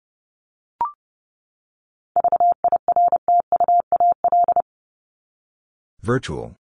Morse code, heard as a keyed tone, about 30 wpm, 700 Hz. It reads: VIRTUAL
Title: LY Suffix - Top 100 Words with Base Words 30wpm